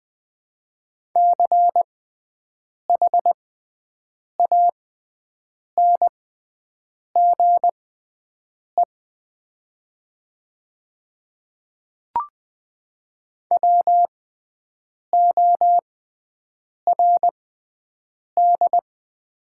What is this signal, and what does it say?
Morse code, heard as a keyed tone, about 20 wpm, spaced as for 7 wpm, 700 Hz. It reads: CHANGE WORD
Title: 6x Character Spacing - Top 300 Words - 20wpm